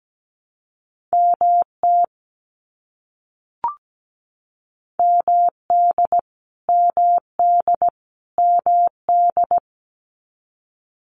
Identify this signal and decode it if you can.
Morse code, heard as a keyed tone, about 17 wpm, 700 Hz. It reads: MT MD MD MD
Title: US States Repeated 3x 17wpm